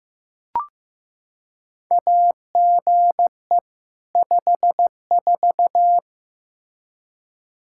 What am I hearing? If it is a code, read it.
Morse code, heard as a keyed tone, about 15 wpm, 700 Hz. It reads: AGE 54
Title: Practice QSO - Exchange 3 - More Info - 15wpm